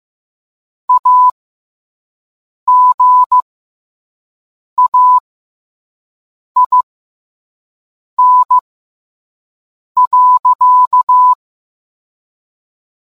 Morse code, heard as a keyed tone, about 15 wpm, spaced as for 5 wpm, 1000 Hz. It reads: AGAIN.